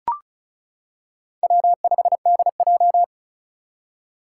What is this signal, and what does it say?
Morse code, heard as a keyed tone, about 36 wpm, 700 Hz. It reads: W5DJ
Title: Four Challenging Letter-Number Combinations 35wpm